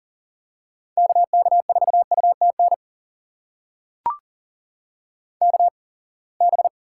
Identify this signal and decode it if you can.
Morse code, heard as a keyed tone, about 40 wpm, spaced as for 28 wpm, 700 Hz. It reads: KK4UTN KB
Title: USA Call Signs - 6 Characters - 8x Character Spacing - 40wpm